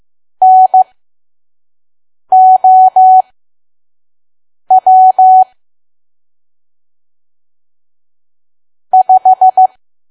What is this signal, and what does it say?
Morse code, heard as a keyed tone, about 15 wpm, spaced as for 5 wpm, 750 Hz. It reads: NOW 5
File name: Archive / 5 WPM files / PM